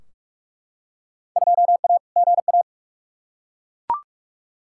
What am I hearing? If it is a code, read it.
Morse code, heard as a keyed tone, about 42 wpm, 700 Hz. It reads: 2A GA
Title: Field Day Exchanges Encore - 45wpm